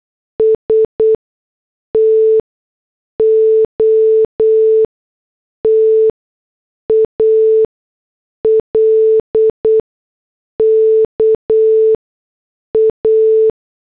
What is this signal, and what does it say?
Morse code, heard as a keyed tone, about 8 wpm, spaced as for 6 wpm, 440 Hz. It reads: S TO TALK A